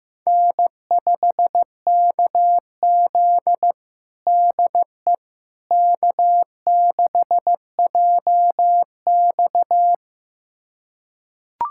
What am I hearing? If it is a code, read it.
Morse code, heard as a keyed tone, about 15 wpm, 700 Hz. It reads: N5KZ DE K6JX
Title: USA Callsign DE Callsign - 4 Characters - 15wpm